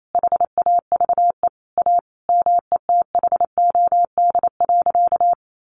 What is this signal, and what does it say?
Morse code, heard as a keyed tone, about 28 wpm, 700 Hz. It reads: HAVE A METHOD.
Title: Sentences from Top 600 Words 28wpm